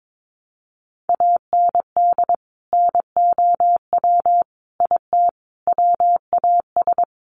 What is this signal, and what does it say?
Morse code, heard as a keyed tone, about 22 wpm, 700 Hz. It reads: AND NOW IT WAS
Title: Sentences from Top 1100 Words 22wpm